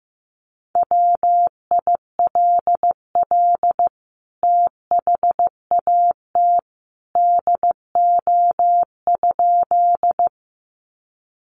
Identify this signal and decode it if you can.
Morse code, heard as a keyed tone, about 15 wpm, 700 Hz. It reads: WILL THAT DO?
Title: Sentences from Top 100 Words 15wpm